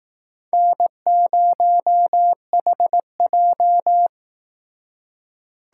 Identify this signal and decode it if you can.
Morse code, heard as a keyed tone, about 18 wpm, 700 Hz. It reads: N0HJ
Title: Speed Racing - USA 1x2 Call Signs 15wpm